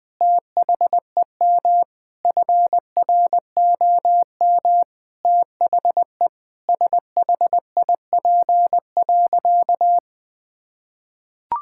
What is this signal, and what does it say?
Morse code, heard as a keyed tone, about 20 wpm, 700 Hz. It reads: THEM FROM THE SHIP.
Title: Sentences from Top 1000 Words - Review Part 5 - 20wpm